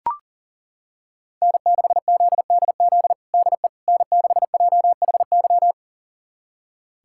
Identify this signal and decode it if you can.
Morse code, heard as a keyed tone, about 37 wpm, 700 Hz. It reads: N6ZDZ DE N6JHY
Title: USA Callsign DE Callsign - 5 Characters - 40wpm